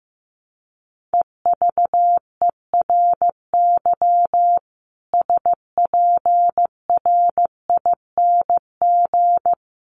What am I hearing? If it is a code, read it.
Morse code, heard as a keyed tone, about 15 wpm, 700 Hz. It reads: EVERY SPRING
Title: Sentences from Top 1200 Words 15wpm